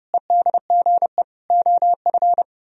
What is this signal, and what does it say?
Morse code, heard as a keyed tone, about 38 wpm, 700 Hz. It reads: EDGE OF
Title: Speed-Racing - Sets of 4 Words 30wpm